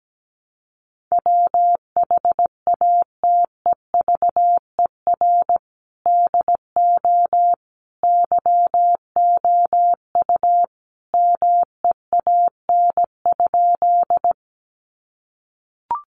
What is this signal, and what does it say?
Morse code, heard as a keyed tone, about 17 wpm, 700 Hz. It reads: WHATEVER DO YOU MEAN?
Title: Sentences from Top 1100 Words 17wpm